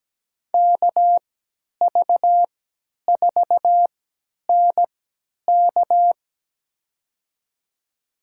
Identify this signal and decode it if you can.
Morse code, heard as a keyed tone, about 17 wpm, spaced as for 10 wpm, 700 Hz. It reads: KV4NK
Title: USA Call Signs - 5 Characters - 3x Character Spacing - 17wpm